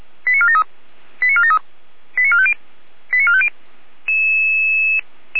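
Five-tone selective call (ZVEI), heard as quick runs of five tones, tones of about 80 ms. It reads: 89473 89473 89450 89450
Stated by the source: ZVEI five-tone call 88074 received by SDR.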